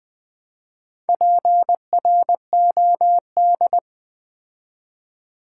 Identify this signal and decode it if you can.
Morse code, heard as a keyed tone, about 20 wpm, 700 Hz. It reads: PROD
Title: Build A Word - 4 Letters At A Time - 8-Letter Words - 20wpm